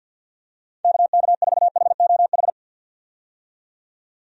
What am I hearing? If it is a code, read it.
Morse code, heard as a keyed tone, about 47 wpm, 700 Hz. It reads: KK4HOH
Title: USA Call Signs - 6 Characters - 50wpm